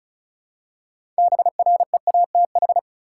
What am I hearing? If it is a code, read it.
Morse code, heard as a keyed tone, about 36 wpm, 700 Hz. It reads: BREATH
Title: Six Letter Words - 35wpm